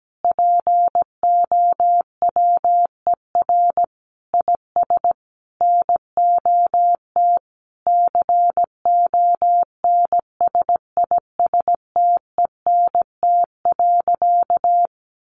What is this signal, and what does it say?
Morse code, heard as a keyed tone, about 17 wpm, 700 Hz. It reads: POWER IS NOT CONSISTENT.
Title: Sentences from Top 1500 Words 17wpm